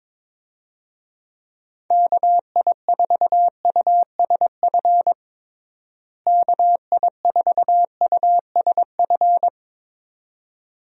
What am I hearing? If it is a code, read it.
Morse code, heard as a keyed tone, about 22 wpm, 700 Hz. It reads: KI4USF KI4USF
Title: USA Call Signs - 6 Characters - Repeated 2x - 22wpm